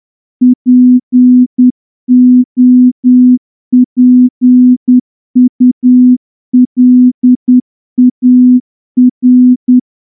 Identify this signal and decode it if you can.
Morse code, heard as a keyed tone, about 9 wpm, 250 Hz. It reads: POPULAR
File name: morse-training-10WPM